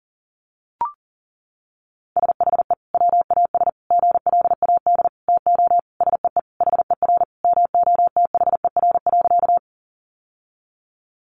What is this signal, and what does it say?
Morse code, heard as a keyed tone, about 40 wpm, 700 Hz. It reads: SHE WAS GLAD TO SEE HER MOTHER.
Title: Sentences from Top 800 Words 40wpm